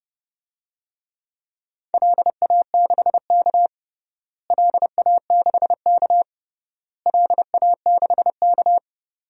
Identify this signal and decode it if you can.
Morse code, heard as a keyed tone, about 32 wpm, 700 Hz. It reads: LA6K LA6K LA6K
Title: DXCC-100 Call Signs - 4 Characters - Repeated 3x - 30wpm